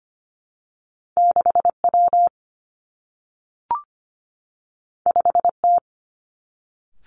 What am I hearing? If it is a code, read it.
Morse code, heard as a keyed tone, about 25 wpm, 700 Hz. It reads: 6 W 5 T
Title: Two Challenging Letter-Number Combinations 25wpm